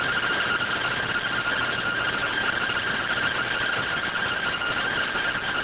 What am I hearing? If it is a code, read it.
Two-tone teleprinter signal, about 170 Hz shift.